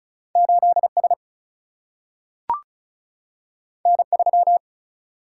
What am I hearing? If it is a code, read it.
Morse code, heard as a keyed tone, about 36 wpm, 700 Hz. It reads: 8 S N 3
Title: Single-Letter Single-Number 35wpm